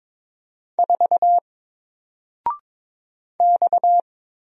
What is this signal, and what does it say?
Morse code, heard as a keyed tone, about 22 wpm, 700 Hz. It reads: 4 X
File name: Single-Letter-Number-22wpm